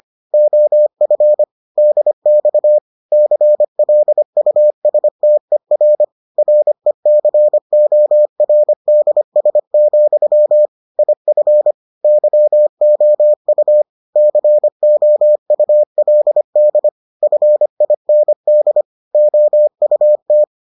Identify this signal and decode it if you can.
Morse code, heard as a keyed tone, about 25 wpm, 600 Hz. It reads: OF DX CLUSTER RECORDS, IF YOU COULD FIND OUT